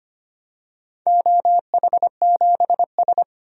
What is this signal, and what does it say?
Morse code, heard as a keyed tone, about 25 wpm, 700 Hz. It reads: OH7S
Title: DXCC-100 Call Signs - 4 Characters - 25wpm